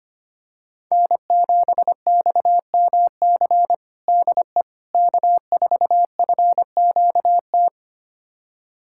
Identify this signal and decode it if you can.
Morse code, heard as a keyed tone, about 25 wpm, 700 Hz. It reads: N7XMC DE K4FQT
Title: USA Callsign DE Callsign 25wpm